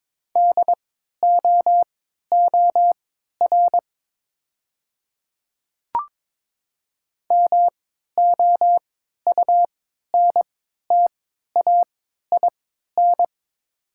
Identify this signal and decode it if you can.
Morse code, heard as a keyed tone, about 22 wpm, spaced as for 12 wpm, 690 Hz. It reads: DOOR MOUNTAIN
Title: 3x Character Spacing - Top 300 Words - 22wpm